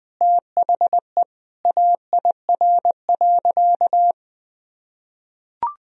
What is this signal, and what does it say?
Morse code, heard as a keyed tone, about 21 wpm, 700 Hz. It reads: THE AIR.
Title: Sentences from Top 1000 Words - Review Part 1 - 20wpm